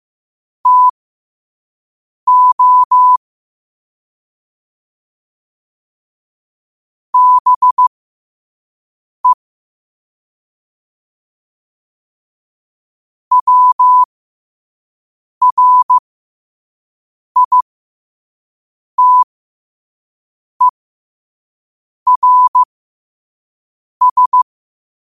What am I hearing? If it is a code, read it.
Morse code, heard as a keyed tone, about 15 wpm, spaced as for 5 wpm, 1000 Hz. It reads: TO BE WRITERS